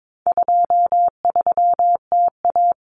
Morse code, heard as a keyed tone, about 22 wpm, 700 Hz. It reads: 23TA
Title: Common Cut Numbers - 4-Digit 22wpm